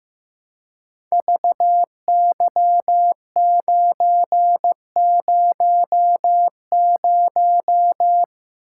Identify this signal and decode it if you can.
Morse code, heard as a keyed tone, about 15 wpm, 700 Hz. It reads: VY900
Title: Code Groups - 5 Letter:Numbers - 15wpm